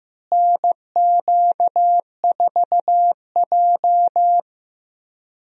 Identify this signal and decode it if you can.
Morse code, heard as a keyed tone, about 15 wpm, 700 Hz. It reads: NQ4J
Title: USA Call Signs - 4 Characters - Repeated 3x - 15wpm